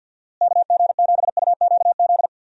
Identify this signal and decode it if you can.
Morse code, heard as a keyed tone, about 47 wpm, 700 Hz. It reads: KG7UQZ
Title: USA Call Signs - 6 Characters - Repeated 2x - 50wpm